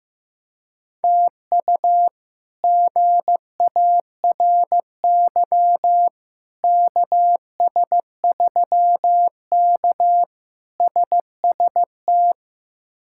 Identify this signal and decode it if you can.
Morse code, heard as a keyed tone, about 15 wpm, 700 Hz. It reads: TU GARY KS3K SST
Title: K1USN SST QSO Practice - 15wpm